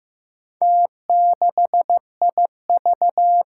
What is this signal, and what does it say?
Morse code, heard as a keyed tone, about 15 wpm, 700 Hz. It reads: T6IV
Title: Four Challenging Letter-Number Combinations 15wpm